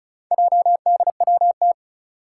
Morse code, heard as a keyed tone, about 36 wpm, 700 Hz. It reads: JDWT
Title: Four Challenging Letter-Number Combinations 35wpm